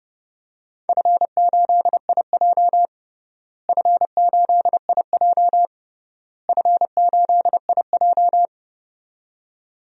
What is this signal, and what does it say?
Morse code, heard as a keyed tone, about 30 wpm, 700 Hz. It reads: F8IJ F8IJ F8IJ
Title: DXCC-100 Call Signs - 4 Characters - Repeated 3x - 30wpm